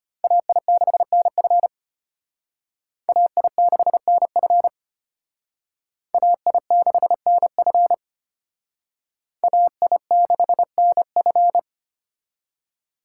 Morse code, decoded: AI6NF AI6NF AI6NF AI6NF

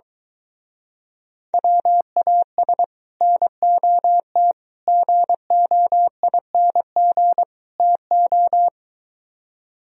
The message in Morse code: WAS NOT GOING TO

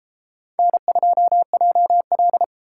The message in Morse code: N2JL

